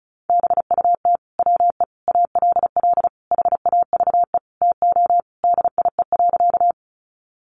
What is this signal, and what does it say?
Morse code, heard as a keyed tone, about 34 wpm, 700 Hz. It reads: BUT WE ALL HAVE TO DIE.
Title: Sentences from Top 1300 Words 35wpm